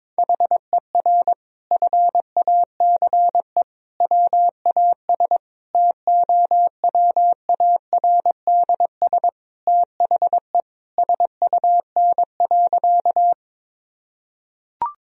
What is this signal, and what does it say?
Morse code, heard as a keyed tone, about 21 wpm, 700 Hz. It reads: HER FACE WAS TOWARDS THE SUN.
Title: Sentences from Top 1000 Words - Review Part 2 - 22wpm